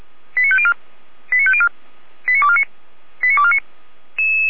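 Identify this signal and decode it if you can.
Five-tone selective call (ZVEI), heard as quick runs of five tones, tones of about 70 ms. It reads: 89504 89504 89249 89249
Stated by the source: ZVEI five-tone call received with an SDR.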